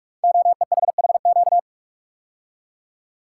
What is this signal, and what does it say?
Morse code, heard as a keyed tone, about 44 wpm, 700 Hz. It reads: OEHHQ
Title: Code Groups - 5 Letters - 45wpm